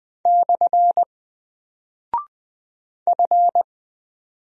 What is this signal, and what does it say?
Morse code, heard as a keyed tone, about 20 wpm, 700 Hz. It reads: / F
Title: Single Character (Letter:Number:Punctuation:Prosign) 20wpm